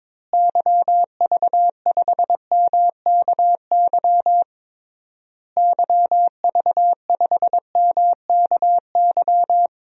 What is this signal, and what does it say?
Morse code, heard as a keyed tone, about 22 wpm, 700 Hz. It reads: YV5MKY YV5MKY
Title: DXCC-100 Call Signs - 6 Characters - Repeated 3x - 22wpm